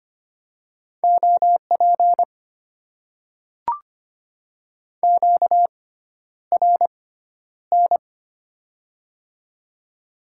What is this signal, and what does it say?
Morse code, heard as a keyed tone, about 25 wpm, spaced as for 11 wpm, 690 Hz. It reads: OP QRN
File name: Top 90 QSO Elements - 6x Character Spacing - 25wpm